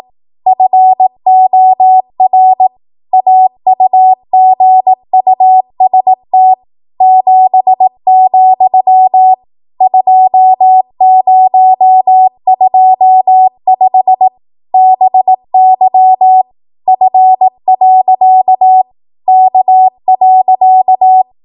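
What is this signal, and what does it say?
Morse code, heard as a keyed tone, about 18 wpm, 750 Hz. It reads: FOR AUGUST 7, 2025 BY F. K.
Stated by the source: ARRL 18 wpm Morse code practice archive transmission.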